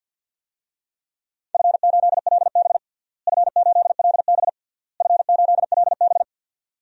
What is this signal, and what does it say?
Morse code, heard as a keyed tone, about 52 wpm, 700 Hz. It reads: F8LB F8LB F8LB
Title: DXCC-100 Call Signs - 4 Characters - Repeated 3x - 50wpm